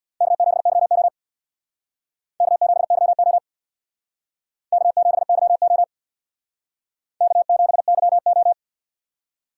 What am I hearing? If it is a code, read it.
Morse code, heard as a keyed tone, about 68 wpm, 700 Hz. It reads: K7YO K7YO K7YO K7YO